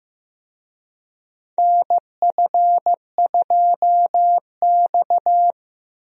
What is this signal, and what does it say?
Morse code, heard as a keyed tone, about 15 wpm, 700 Hz. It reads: NF2X
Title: USA Call Signs - 4 Characters - Repeated 2x - 15wpm